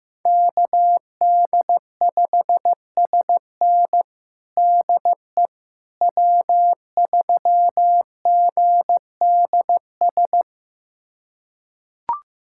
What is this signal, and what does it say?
Morse code, heard as a keyed tone, about 16 wpm, 700 Hz. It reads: KD5SN DE W3GDS
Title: USA Callsign DE Callsign 15wpm